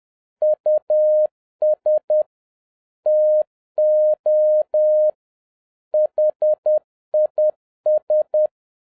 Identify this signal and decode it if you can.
Morse code, heard as a keyed tone, about 10 wpm, 610 Hz. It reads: US TO HIS
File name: 2011-07-mp3 / n46 JOHN K3TN 10wpm